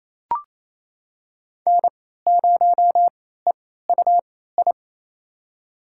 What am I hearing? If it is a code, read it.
Morse code, heard as a keyed tone, about 28 wpm, spaced as for 16 wpm, 700 Hz. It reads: N0EUI
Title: USA Call Signs - 3x Character Spacing - 28wpm